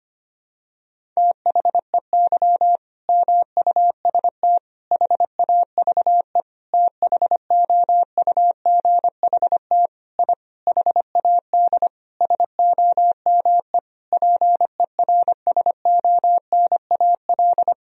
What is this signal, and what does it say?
Morse code, heard as a keyed tone, about 25 wpm, 700 Hz. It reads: THEY MUST HAVE THOUGHT I HAD SOME PERSONAL
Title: Sentences from Top 1000 Words - Review Part 4 - 25wpm